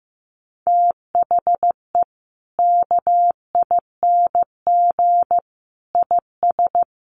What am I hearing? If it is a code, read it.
Morse code, heard as a keyed tone, about 15 wpm, 700 Hz. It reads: THE KING IS
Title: Sentences from Top 600 Words 15wpm